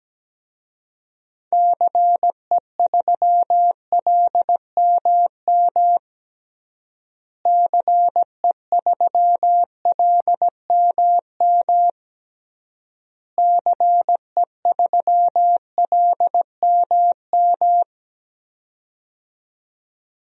Morse code, heard as a keyed tone, about 17 wpm, 700 Hz. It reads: CE3LMM CE3LMM CE3LMM
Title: DXCC-100 Call Signs - 6 Characters - Repeated 3x - 17wpm